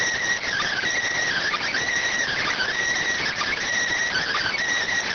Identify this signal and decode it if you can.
Emergency Alert System (SAME) tones, heard as harsh buzzing bursts. It is heard from 0.4 to 4.5 s.